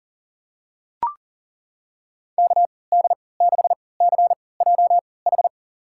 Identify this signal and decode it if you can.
Morse code, heard as a keyed tone, about 37 wpm, spaced as for 22 wpm, 700 Hz. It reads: KD6CJH
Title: USA Call Signs - 6 Characters - 3x Character Spacing - 40wpm